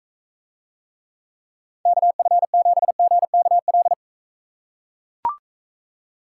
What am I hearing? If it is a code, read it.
Morse code, heard as a keyed tone, about 40 wpm, 700 Hz. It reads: KF7GKL